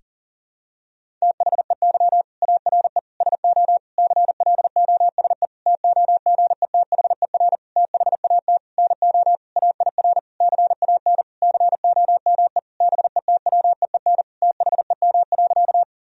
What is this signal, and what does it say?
Morse code, heard as a keyed tone, about 40 wpm, 700 Hz. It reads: THEY ARE SO CLOSE TOGETHER THAT NO AIR CAN COME BETWEEN THEM.
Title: Sentences from Top 1000 Words - Review Part 1 - 40wpm